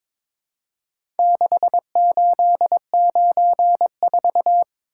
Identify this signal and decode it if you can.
Morse code, heard as a keyed tone, about 23 wpm, 700 Hz. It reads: 6894
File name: practice / 22wpm / Four Letter-Number Combinations - 22wpm